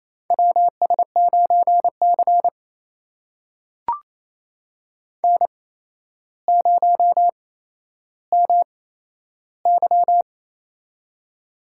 Morse code, heard as a keyed tone, about 28 wpm, spaced as for 12 wpm, 700 Hz. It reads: WS9C N0MY
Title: USA Call Signs - 4 Characters - 8x Character Spacing - 28wpm